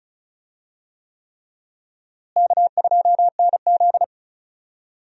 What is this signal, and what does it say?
Morse code, heard as a keyed tone, about 36 wpm, 700 Hz. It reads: K2NZ
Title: USA Call Signs - 4 Characters - Repeated 3x - 35wpm